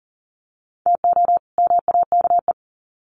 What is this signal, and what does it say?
Morse code, heard as a keyed tone, about 38 wpm, 700 Hz. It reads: TO MAKE
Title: Sets of 2 Words 40wpm